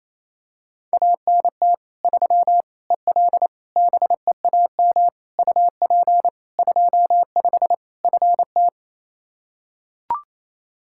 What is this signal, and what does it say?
Morse code, heard as a keyed tone, about 27 wpm, 700 Hz. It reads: ANT 3 EL BEAM UP 25 FT
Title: Practice QSO - Exchange 2 - Info - 28wpm